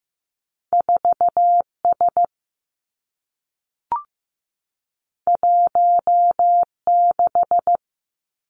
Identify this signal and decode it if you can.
Morse code, heard as a keyed tone, about 15 wpm, 700 Hz. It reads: 4 S 1 6
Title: Two Challenging Letter-Number Combinations 15wpm